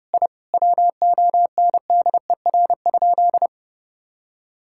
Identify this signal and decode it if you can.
Morse code, heard as a keyed tone, about 30 wpm, 700 Hz. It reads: I WONDER?
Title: Sentences from Top 1000 Words - Review Part 5 - 30wpm